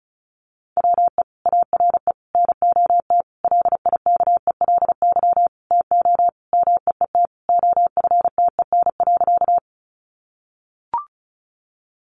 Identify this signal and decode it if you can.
Morse code, heard as a keyed tone, about 34 wpm, 700 Hz. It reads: WE ARE NOT LIKELY TO MEET OFTEN.
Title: Sentences from Top 1100 Words 35wpm